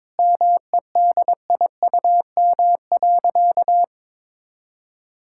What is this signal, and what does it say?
Morse code, heard as a keyed tone, about 22 wpm, 700 Hz. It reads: MEDIUM.
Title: Sentences from Top 1000 Words - Review Part 6 - 22wpm